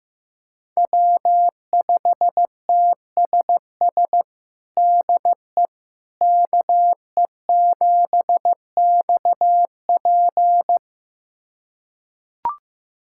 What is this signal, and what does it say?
Morse code, heard as a keyed tone, about 15 wpm, 700 Hz. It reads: W5TSS DE KE7XP
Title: USA Callsign DE Callsign 15wpm